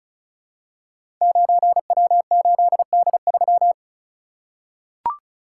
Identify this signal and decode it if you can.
Morse code, heard as a keyed tone, about 36 wpm, 700 Hz. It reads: 9W8D3
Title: Code Groups - 5 Letter:Numbers - 35wpm